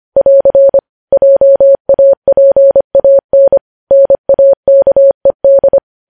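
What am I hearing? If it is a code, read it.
Morse code, heard as a keyed tone, about 25 wpm, 550 Hz. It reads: <AR> JAPAN NAKED
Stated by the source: Morse code news headlines sent at 25 wpm.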